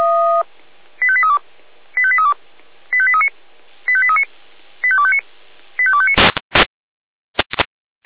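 Five-tone selective call (ZVEI), heard as quick runs of five tones, tones of about 70 ms. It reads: 85832 85832 85839 85839 85259 85259
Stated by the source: ZVEI five-tone selective call received on SDR.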